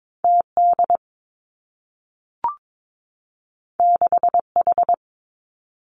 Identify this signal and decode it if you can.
Morse code, heard as a keyed tone, about 22 wpm, 700 Hz. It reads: T D 6 H
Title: Two Challenging Letter-Number Combinations 22wpm